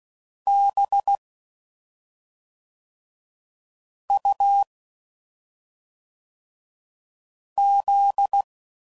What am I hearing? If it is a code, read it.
Morse code, heard as a keyed tone, about 16 wpm, 800 Hz. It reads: B U Z
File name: Suffixes / Special ING's / SlowToFaster ING's